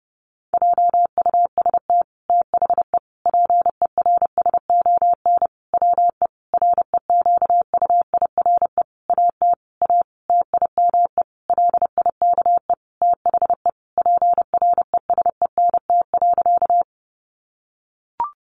Morse code, heard as a keyed tone, about 30 wpm, 700 Hz. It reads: JUST THE PERSON WE REQUIRE AT A TIME LIKE THE PRESENT.
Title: Sentences from Top 900 Words 30wpm